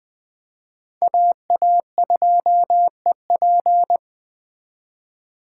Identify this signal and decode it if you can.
Morse code, heard as a keyed tone, about 19 wpm, 700 Hz. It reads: AA2EP